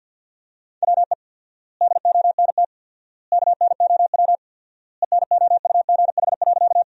Morse code, heard as a keyed tone, about 50 wpm, 700 Hz. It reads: WE DONT KNOW ENOUGH.
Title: 4x Word-Spacing - Sentences from Top 400 Words - 50wpm